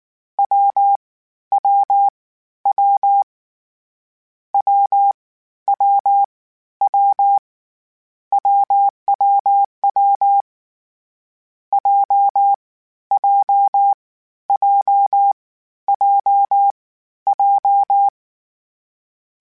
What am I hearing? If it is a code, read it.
Morse code, heard as a keyed tone, about 19 wpm, spaced as for 12 wpm, 800 Hz. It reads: WWW WWW WWW JJJJJ
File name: Difficult Combos / Difficult Combos-mp3s / AWJ1 @18